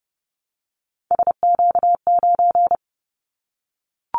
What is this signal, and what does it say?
Morse code, heard as a keyed tone, about 30 wpm, 700 Hz. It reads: SQ9
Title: Callsign Prefixes for Top 25 DXCC - Repeated 3x - 30wpm